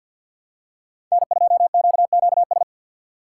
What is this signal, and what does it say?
Morse code, heard as a keyed tone, about 48 wpm, 690 Hz. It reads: N2QQS
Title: USA Call Signs - 3x Character Spacing - 50wpm